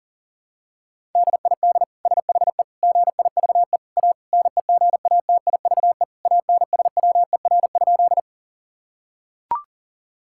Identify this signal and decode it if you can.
Morse code, heard as a keyed tone, about 40 wpm, 700 Hz. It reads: DID SHE GIVE A NEGATIVE ANSWER?
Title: Sentences from Top 1000 Words - Review Part 6 - 40wpm